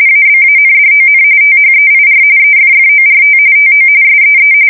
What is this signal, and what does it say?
RTTY, about 45.45 baud. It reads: ABANDON THEIR EFFORT TO SEE
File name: fsk170 / USS-New-Jersey